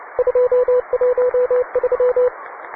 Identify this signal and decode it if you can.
Morse code, heard as a keyed tone, about 29 wpm, 500 Hz, with radio noise. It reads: 213